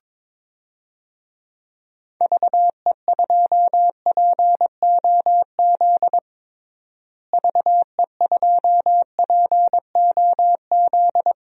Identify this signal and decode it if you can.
Morse code, heard as a keyed tone, about 21 wpm, 700 Hz. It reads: VE2POZ VE2POZ
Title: DXCC-100 Call Signs - 6 Characters - Repeated 2x - 22wpm